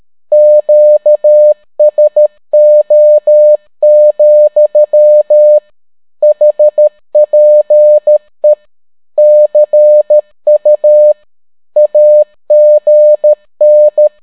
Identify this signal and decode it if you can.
Morse code, heard as a keyed tone, about 13 wpm, 600 Hz. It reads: QSO, HPE CU AGN